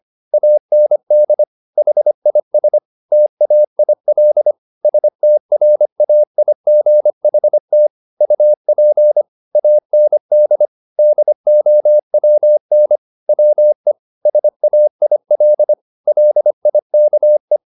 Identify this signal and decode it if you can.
Morse code, heard as a keyed tone, about 25 wpm, 600 Hz. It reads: AND HIS TAIL STRAIGHT UP AND DOWN WE SAIL LIKE